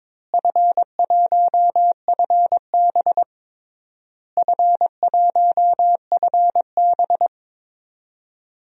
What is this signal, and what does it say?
Morse code, heard as a keyed tone, about 22 wpm, 700 Hz. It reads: F1FB F1FB
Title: DXCC-100 Call Signs - 4 Characters - Repeated 3x - 22wpm